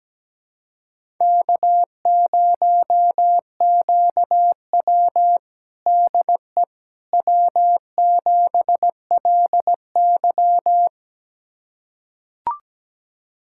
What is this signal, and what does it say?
Morse code, heard as a keyed tone, about 17 wpm, 700 Hz. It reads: K0QW DE W7LY
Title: USA Callsign DE Callsign - 4 Characters - 17wpm